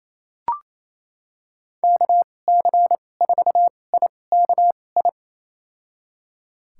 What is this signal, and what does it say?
Morse code, heard as a keyed tone, about 28 wpm, spaced as for 20 wpm, 700 Hz. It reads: KC4IKI
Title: USA Call Signs - 6 Characters - 2x Character Spacing - 28wpm